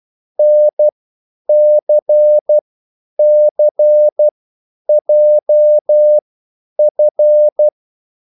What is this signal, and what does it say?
Morse code, heard as a keyed tone, about 12 wpm, spaced as for 9 wpm, 600 Hz. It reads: NCCJF
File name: QSL / CW lesson